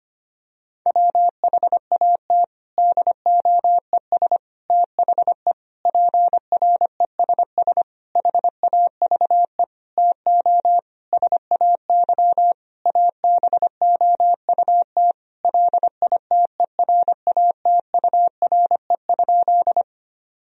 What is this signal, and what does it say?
Morse code, heard as a keyed tone, about 25 wpm, 700 Hz. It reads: WHAT DOES THE PRESS HAVE TO SAY ABOUT LITERATURE?
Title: Sentences from Top 1000 Words - Review Part 3 - 25wpm